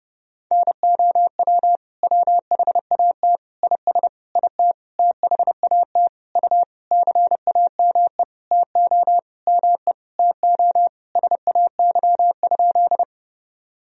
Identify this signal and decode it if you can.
Morse code, heard as a keyed tone, about 30 wpm, 700 Hz. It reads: NOW WHAT IS IT THAT U CAME TO ME TO SAY?
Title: Sentences from Top 200 Words and Abbreviations 30wpm